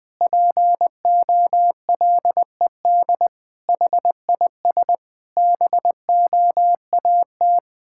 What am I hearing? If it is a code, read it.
Morse code, heard as a keyed tone, about 21 wpm, 700 Hz. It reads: POLED HIS BOAT
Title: Pangram Challenge v2 - 20wpm